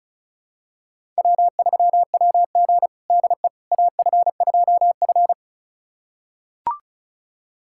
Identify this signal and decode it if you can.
Morse code, heard as a keyed tone, about 36 wpm, 700 Hz. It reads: W3WG DE AF2F
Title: USA Callsign DE Callsign - 4 Characters - 35wpm